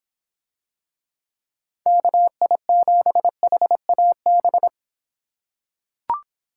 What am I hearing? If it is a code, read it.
Morse code, heard as a keyed tone, about 26 wpm, 700 Hz. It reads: KI7HAB